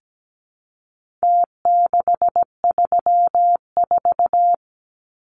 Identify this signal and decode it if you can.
Morse code, heard as a keyed tone, about 17 wpm, 700 Hz. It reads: T634
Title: Numbers - Contest Style - 4-Digit 17wpm